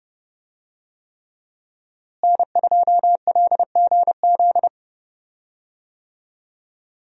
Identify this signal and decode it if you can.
Morse code, heard as a keyed tone, about 30 wpm, 700 Hz. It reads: N2LGZ
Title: USA Call Signs - 5 Characters - Repeated 3x - 30wpm